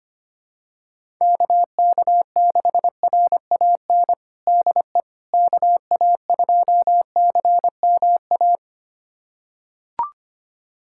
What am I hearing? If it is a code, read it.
Morse code, heard as a keyed tone, about 25 wpm, 700 Hz. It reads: KK6RAN DE KA2CMA
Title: USA Callsign DE Callsign - 6 Characters - 25wpm